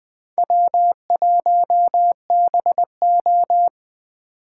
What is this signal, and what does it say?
Morse code, heard as a keyed tone, about 24 wpm, 700 Hz. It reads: W1BO
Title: Speed Racing - USA Call Signs 20wpm